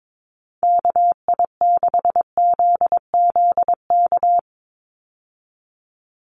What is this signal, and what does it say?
Morse code, heard as a keyed tone, about 22 wpm, 700 Hz. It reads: KI6ZZK
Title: USA Call Signs 22wpm